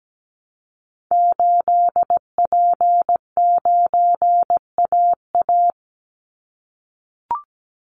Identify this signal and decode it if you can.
Morse code, heard as a keyed tone, about 17 wpm, 700 Hz. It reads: 8P9AA
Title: Call Signs - 17wpm